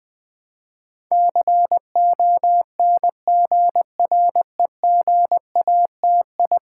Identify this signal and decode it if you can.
Morse code, heard as a keyed tone, about 20 wpm, 700 Hz. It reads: CONGREGATI
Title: Build A Word - 5 Letters At A Time - 12-Letter Words - 20wpm